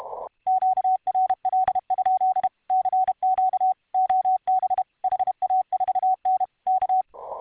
Morse code, heard as a keyed tone, about 31 wpm, 750 Hz. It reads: QRL? CQ OB HA4N K